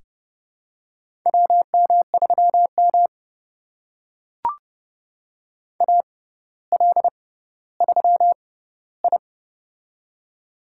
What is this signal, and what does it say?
Morse code, heard as a keyed tone, about 29 wpm, spaced as for 15 wpm, 700 Hz. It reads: WM3M AL3I
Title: USA Call Signs - 4 Characters - 6x Character Spacing - 30wpm